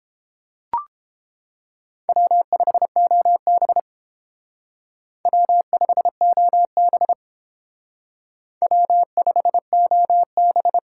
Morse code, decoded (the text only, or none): W5OB W5OB W5OB